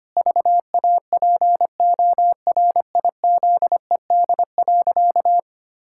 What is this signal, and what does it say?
Morse code, heard as a keyed tone, about 25 wpm, 700 Hz. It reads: VAPORIZED.
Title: Pangram Challenge v2 - 25wpm